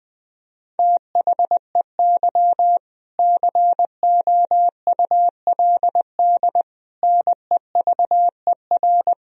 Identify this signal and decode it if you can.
Morse code, heard as a keyed tone, about 21 wpm, 700 Hz. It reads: THEY COULD NEVER
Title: Sentences from Top 1000 Words - Review Part 4 - 20wpm